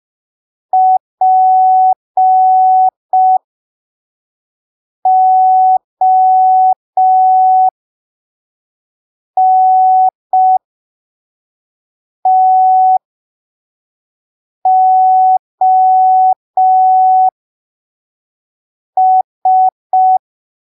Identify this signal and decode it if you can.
Morse code, heard as a keyed tone, about 5 wpm, spaced as for 3 wpm, 750 Hz. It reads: PONTOS
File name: arquivos / texto / texto 04 5ppm